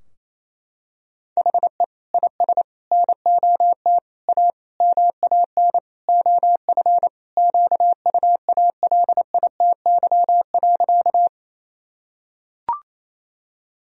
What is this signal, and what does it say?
Morse code, heard as a keyed tone, about 28 wpm, 700 Hz. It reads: HE IS NOT A MAN OF QUALITY.
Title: Sentences from Top 1000 Words - Review Part 4 - 28wpm